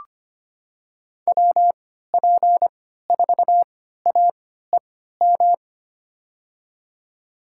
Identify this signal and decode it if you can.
Morse code, heard as a keyed tone, about 25 wpm, spaced as for 14 wpm, 700 Hz. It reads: WP4AEM
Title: USA Call Signs - 6 Characters - 3x Character Spacing - 25wpm